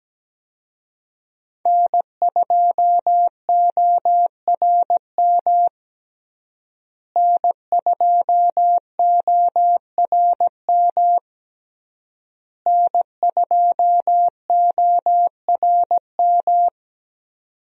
Morse code, heard as a keyed tone, about 17 wpm, 700 Hz. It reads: N2ORM N2ORM N2ORM
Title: USA Call Signs - 5 Characters - Repeated 3x - 17wpm